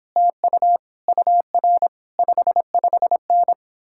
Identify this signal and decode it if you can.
Morse code, heard as a keyed tone, about 29 wpm, 700 Hz. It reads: TU UR 55N